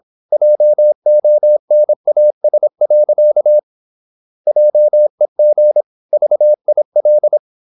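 Morse code, heard as a keyed tone, about 26 wpm, 600 Hz. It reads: JONAS. JEG VIL
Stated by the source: High-speed (QRQ) Morse at 26 wpm.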